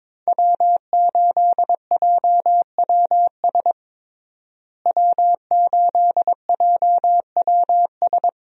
Morse code, heard as a keyed tone, about 22 wpm, 700 Hz. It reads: W8JWS W8JWS
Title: USA Call Signs - 5 Characters - Repeated 2x - 22wpm